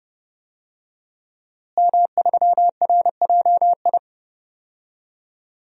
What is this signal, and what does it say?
Morse code, heard as a keyed tone, about 30 wpm, 700 Hz. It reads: M3RJI
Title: DXCC-100 Call Signs - 5 Characters - Repeated 2x - 30wpm